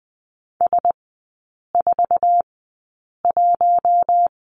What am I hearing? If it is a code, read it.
Morse code, heard as a keyed tone, about 20 wpm, spaced as for 8 wpm, 700 Hz. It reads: S41
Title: 047 Lesson 9e - Number 4 - Three Characters 20wpm